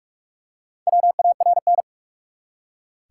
Morse code, heard as a keyed tone, about 44 wpm, 700 Hz. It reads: WARN
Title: Four Letter Words - 45wpm